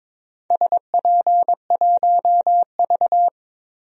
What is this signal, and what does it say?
Morse code, heard as a keyed tone, about 22 wpm, 700 Hz. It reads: SP1V
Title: DXCC-100 Call Signs - 4 Characters - Repeated 2x - 22wpm